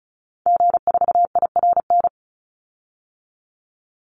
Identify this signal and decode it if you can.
Morse code, heard as a keyed tone, about 34 wpm, 700 Hz. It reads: G4IRN
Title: Call Signs - 35wpm